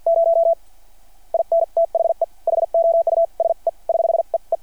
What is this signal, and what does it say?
Morse code, heard as a keyed tone, about 53 wpm, spaced as for 44 wpm, 690 Hz. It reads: 0 IN THE HOUSE <HH> E E
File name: archived / CRSnet